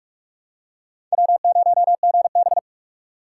Morse code, heard as a keyed tone, about 46 wpm, 700 Hz. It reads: W0GB
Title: USA Call Signs - 4x Character Spacing - 45wpm